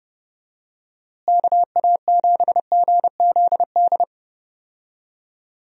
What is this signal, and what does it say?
Morse code, heard as a keyed tone, about 30 wpm, 700 Hz. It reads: KA7GZD
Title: USA Call Signs - 6 Characters - Repeated 2x - 30wpm